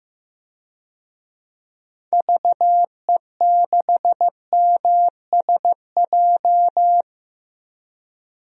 Morse code, heard as a keyed tone, about 15 wpm, 700 Hz. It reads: VE6MSJ
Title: DXCC-100 Call Signs - 6 Characters - Repeated 3x - 15wpm